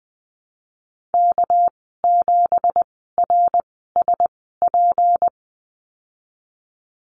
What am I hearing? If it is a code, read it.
Morse code, heard as a keyed tone, about 19 wpm, spaced as for 14 wpm, 700 Hz. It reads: K7RSP
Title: 238 Lesson 38g - Letter q - Call Signs 20wpm